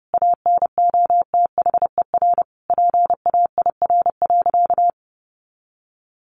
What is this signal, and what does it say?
Morse code, heard as a keyed tone, about 28 wpm, 700 Hz. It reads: ANOTHER PAIR.
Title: Sentences from Top 1200 Words 30wpm